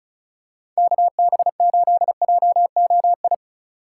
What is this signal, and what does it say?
Morse code, heard as a keyed tone, about 36 wpm, 700 Hz. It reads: KB8JOI
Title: USA Call Signs - 6 Characters - 2x Character Spacing - 35wpm